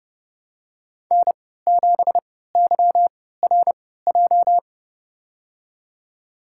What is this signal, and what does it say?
Morse code, heard as a keyed tone, about 30 wpm, spaced as for 17 wpm, 700 Hz. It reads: N7YRJ
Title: USA Call Signs - 3x Character Spacing - 30wpm